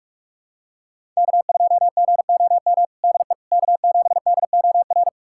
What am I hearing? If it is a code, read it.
Morse code, heard as a keyed tone, about 46 wpm, 700 Hz. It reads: K2GOM DE K7DOR
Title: USA Callsign DE Callsign - 5 Characters - 45wpm